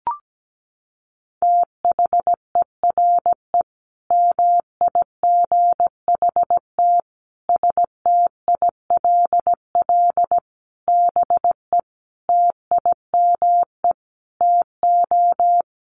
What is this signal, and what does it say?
Morse code, heard as a keyed tone, about 17 wpm, 700 Hz. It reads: THERE MIGHT STILL BE TIME TO
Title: Sentences from Top 1100 Words 17wpm